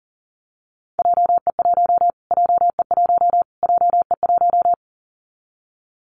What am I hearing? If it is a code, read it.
Morse code, heard as a keyed tone, about 40 wpm, 700 Hz. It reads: JE1 JE1 JE1
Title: Callsign Prefixes for Top 25 DXCC - Repeated 3x - 40wpm